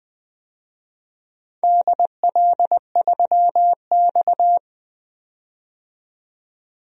Morse code, heard as a keyed tone, about 20 wpm, 700 Hz. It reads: DL3X
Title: DXCC-100 Call Signs - 4 Characters - 20wpm